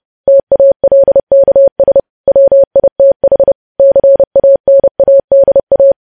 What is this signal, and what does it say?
Morse code, heard as a keyed tone, about 30 wpm, 550 Hz. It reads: TALKS WITH CANADA
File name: podcast / News headlines-1x-30x30